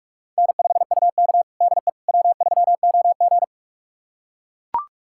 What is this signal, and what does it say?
Morse code, heard as a keyed tone, about 44 wpm, 690 Hz. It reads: N5UK DE W3OG
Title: USA Callsign DE Callsign - 4 Characters - 45wpm